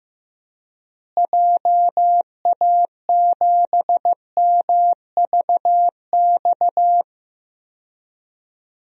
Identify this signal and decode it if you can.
Morse code, heard as a keyed tone, about 15 wpm, 700 Hz. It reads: JA7MVX
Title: DXCC-100 Call Signs - 6 Characters - Repeated 3x - 15wpm